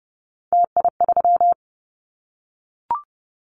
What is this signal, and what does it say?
Morse code, heard as a keyed tone, about 30 wpm, 700 Hz. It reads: TI3
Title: Callsign Prefixes for Top 100 DXCC - 30wpm